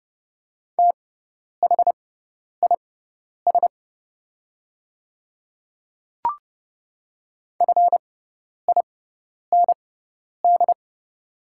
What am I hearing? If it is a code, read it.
Morse code, heard as a keyed tone, about 30 wpm, spaced as for 10 wpm, 700 Hz. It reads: THIS FIND
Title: 6x Character Spacing - Top 100 Words - 30wpm